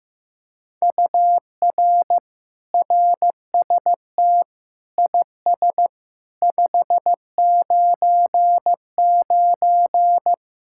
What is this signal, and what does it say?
Morse code, heard as a keyed tone, about 16 wpm, 700 Hz. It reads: UR RST IS 599
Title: Speed-Racing Common QSO Phrases Encore - 15wpm